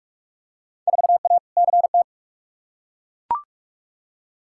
Morse code, heard as a keyed tone, about 42 wpm, 700 Hz. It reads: 4A CT
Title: Field Day Exchanges Encore - 45wpm